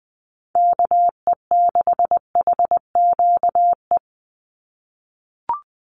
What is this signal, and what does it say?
Morse code, heard as a keyed tone, about 20 wpm, 700 Hz. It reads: KE6HQE
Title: USA Call Signs 20wpm